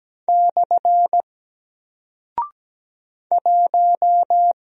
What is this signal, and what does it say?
Morse code, heard as a keyed tone, about 17 wpm, 700 Hz. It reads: / 1
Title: Single Character (Letter:Number:Punctuation:Prosign) Alt 17wpm